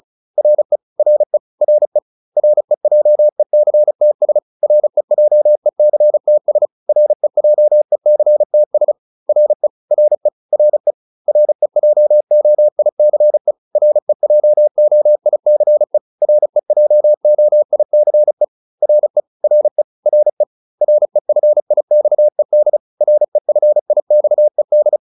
Morse code, heard as a keyed tone, about 35 wpm, 600 Hz. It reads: RE RE RE REJECTS REJECTS REJECTS RE RE RE REJOICE REJOICE REJOICE RE RE RE REFIXED REFIXED